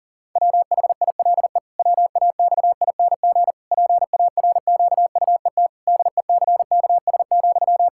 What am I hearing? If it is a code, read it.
Morse code, heard as a keyed tone, about 40 wpm, 700 Hz. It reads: WHILE WAXING PARQUET DECKS,
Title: Pangram Challenge v2 - 40wpm